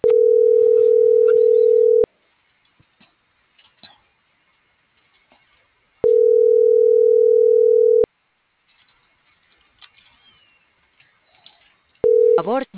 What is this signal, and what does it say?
Ringback tone (North America), 440 + 480 Hz.